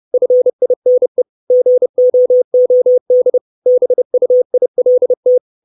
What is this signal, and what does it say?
Morse code, heard as a keyed tone, about 30 wpm, 500 Hz. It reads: FINE GOOD BUILT
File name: 632 Words 30WPM